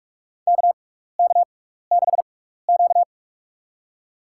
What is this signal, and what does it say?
Morse code, heard as a keyed tone, about 46 wpm, spaced as for 15 wpm, 700 Hz. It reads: KK6Q
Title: USA Call Signs - 4 Characters - 6x Character Spacing - 45wpm